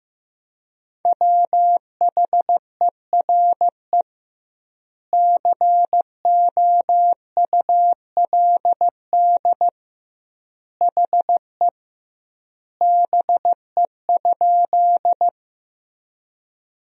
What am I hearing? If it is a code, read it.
Morse code, heard as a keyed tone, about 15 wpm, 700 Hz. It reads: WHERE COULD HE BE?
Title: 2x Word-Spacing - Sentences from Top 200 Words - 15wpm